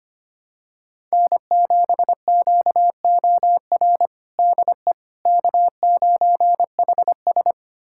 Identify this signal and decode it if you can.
Morse code, heard as a keyed tone, about 25 wpm, 700 Hz. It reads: N7QOR DE K9HS
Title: USA Callsign DE Callsign 25wpm